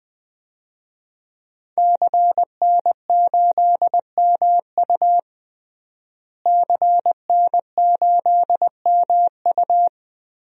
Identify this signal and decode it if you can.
Morse code, heard as a keyed tone, about 21 wpm, 700 Hz. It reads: CN8MU CN8MU
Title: DXCC-100 Call Signs - 5 Characters - Repeated 3x - 20wpm